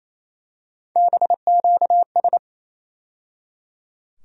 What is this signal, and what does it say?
Morse code, heard as a keyed tone, about 27 wpm, 700 Hz. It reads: BQS
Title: USA Call Sign Postfixes - 3 Letters 28wpm